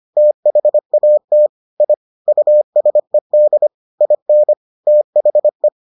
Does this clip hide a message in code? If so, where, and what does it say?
Yes: Morse code from 0.2 to 5.7 s, heard as a keyed tone, about 25 wpm, 600 Hz. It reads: THAT I USED IN THE